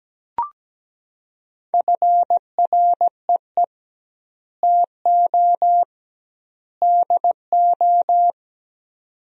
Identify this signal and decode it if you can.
Morse code, heard as a keyed tone, about 17 wpm, 700 Hz. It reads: FREE TO DO
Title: 2x Word-Spacing - Sentences from Top 500 Words - 17wpm